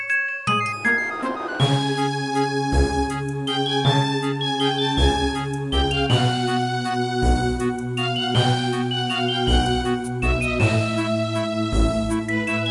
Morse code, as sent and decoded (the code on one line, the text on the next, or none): none